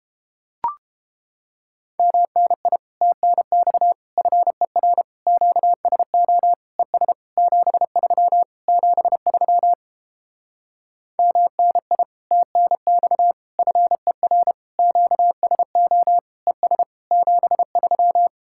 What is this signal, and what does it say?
Morse code, heard as a keyed tone, about 32 wpm, 700 Hz. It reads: MNI TNX FER QSO ES 73 73 MNI TNX FER QSO ES 73